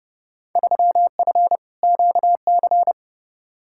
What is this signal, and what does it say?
Morse code, heard as a keyed tone, about 32 wpm, 700 Hz. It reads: 3F QC
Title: Field Day Exchanges - 30wpm